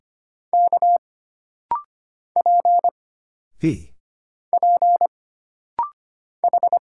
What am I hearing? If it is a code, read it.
Morse code, heard as a keyed tone, about 25 wpm, 700 Hz. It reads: K P P H
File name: Single Character (Letter:Number:Punctuation:Prosign) - Rapid-Fire - 25wpm